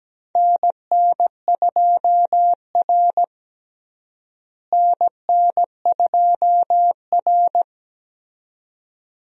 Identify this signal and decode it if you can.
Morse code, heard as a keyed tone, about 17 wpm, 700 Hz. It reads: NN2R NN2R
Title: USA Call Signs - 4 Characters - Repeated 2x - 17wpm